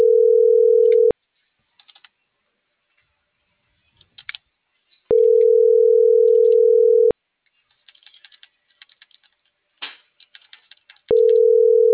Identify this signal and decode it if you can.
Ringback tone (North America), 440 + 480 Hz.